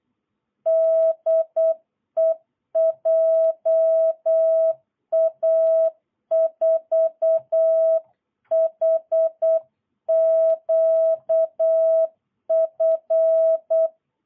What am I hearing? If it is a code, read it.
Morse code, heard as a keyed tone, about 8 wpm, 650 Hz. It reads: DE JA4HQF
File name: jh8cbh / 2022cw